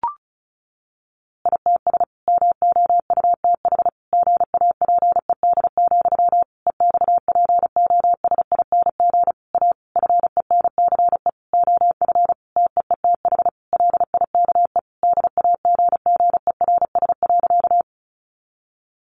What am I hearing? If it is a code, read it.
Morse code, heard as a keyed tone, about 34 wpm, 700 Hz. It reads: ITS MOUTH GAPED, EXPOSING A FENCE OF TEETH LIKE DAGGERS.
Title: A Sound Of Thunder 35wpm